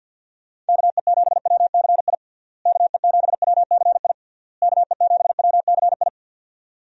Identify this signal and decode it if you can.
Morse code, heard as a keyed tone, about 48 wpm, 700 Hz. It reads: KE7WCI KE7WCI KE7WCI
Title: USA Call Signs - 6 Characters - Repeated 3x - 50wpm